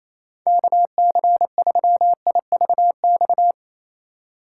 Morse code, heard as a keyed tone, about 28 wpm, 700 Hz. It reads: KC3IVX
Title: USA Call Signs - 6 Characters - 8x Character Spacing - 28wpm